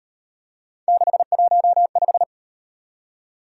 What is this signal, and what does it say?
Morse code, heard as a keyed tone, about 38 wpm, 700 Hz. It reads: 615